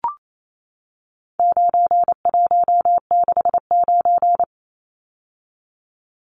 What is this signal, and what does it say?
Morse code, heard as a keyed tone, about 28 wpm, 700 Hz. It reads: 9169
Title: Numbers - 4-Digit 28wpm